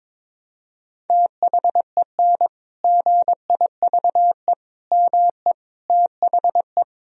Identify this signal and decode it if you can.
Morse code, heard as a keyed tone, about 22 wpm, 700 Hz. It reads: THEN GIVE ME THE
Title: Sentences from Top 1000 Words - Review Part 1 - 22wpm